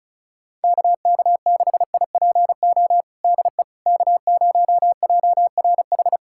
Morse code, heard as a keyed tone, about 36 wpm, 700 Hz. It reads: KK6IPO DE K0JRH
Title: USA Callsign DE Callsign 35wpm